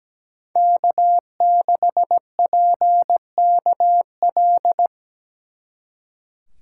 Morse code, heard as a keyed tone, about 20 wpm, 700 Hz. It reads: K6PKL
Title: Speed Racing - USA 1x3 Call Signs 17wpm